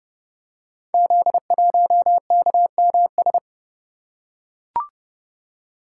Morse code, heard as a keyed tone, about 28 wpm, 700 Hz. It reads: Z1KMS
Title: Code Groups - 5 Letter:Numbers - 30wpm